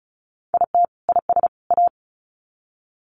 Morse code, heard as a keyed tone, about 34 wpm, 700 Hz. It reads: IT IS A
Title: Sets of 3 Words 35wpm